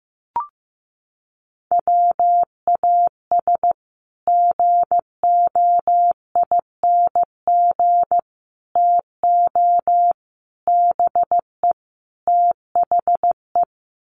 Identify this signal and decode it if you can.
Morse code, heard as a keyed tone, about 15 wpm, 700 Hz. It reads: WAS GOING TO BE THE
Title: Sets of 5 Words Encore2 15wpm